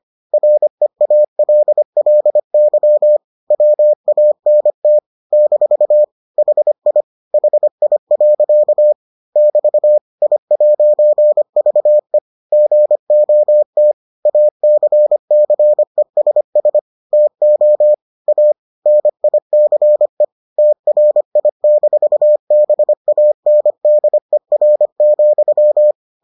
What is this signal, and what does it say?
Morse code, heard as a keyed tone, about 25 wpm, 600 Hz. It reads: REALLY WANT - HI HI. <BT> I'VE GOT ACCESS TO A NICE TRI-BANDER,